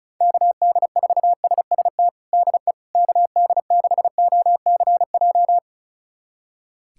Morse code, heard as a keyed tone, about 36 wpm, 700 Hz. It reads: KD4SST DE KD6OCJ
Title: USA Callsign DE Callsign - 6 Characters - 35wpm